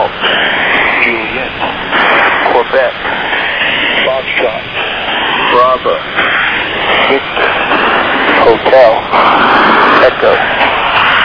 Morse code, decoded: none

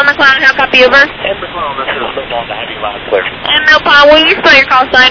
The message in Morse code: none

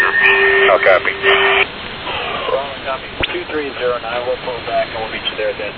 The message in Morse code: none

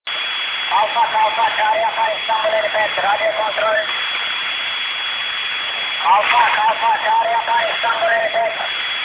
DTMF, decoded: none

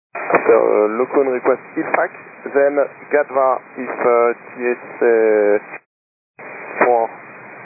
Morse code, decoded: none

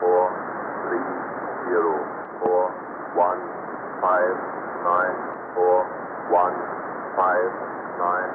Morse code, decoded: none

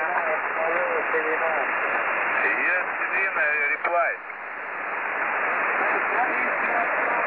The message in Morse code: none

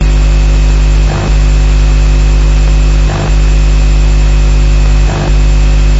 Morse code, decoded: none

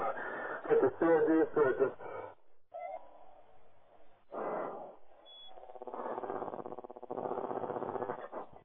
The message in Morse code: none

